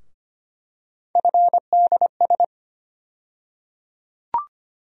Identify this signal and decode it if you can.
Morse code, heard as a keyed tone, about 25 wpm, 700 Hz. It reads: FDS
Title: USA Call Sign Postfixes - 3 Letters 25wpm